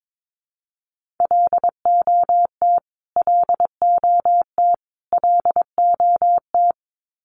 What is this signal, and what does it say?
Morse code, heard as a keyed tone, about 22 wpm, 700 Hz. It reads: LOT LOT LOT
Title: Top 500 Words Repeated 22wpm